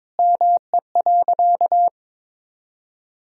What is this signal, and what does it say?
Morse code, heard as a keyed tone, about 22 wpm, 700 Hz. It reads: ME.
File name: practice / 22wpm / Sentences with 8-Word Phrases - 22wpm